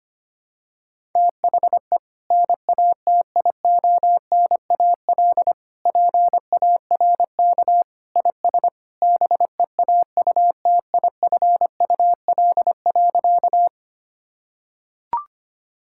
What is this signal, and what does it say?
Morse code, heard as a keyed tone, about 25 wpm, 700 Hz. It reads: THE NATIONAL PARK IS BEAUTIFUL.
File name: practice / 25wpm / Sentences with 2-Word Phrases - 25wpm